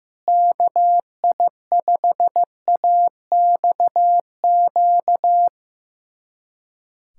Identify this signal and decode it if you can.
Morse code, heard as a keyed tone, about 16 wpm, 700 Hz. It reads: KI5AXQ
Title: USA Callsign DE Callsign - 6 Characters - 15wpm